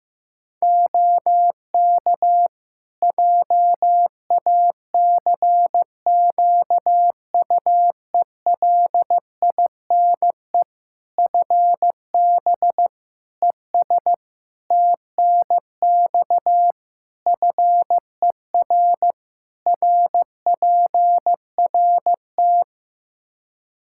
Morse code, heard as a keyed tone, about 16 wpm, 700 Hz. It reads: OK JACQUELINE FB ES TNX FER RPRT
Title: K1USN SST QSO Practice 15wpm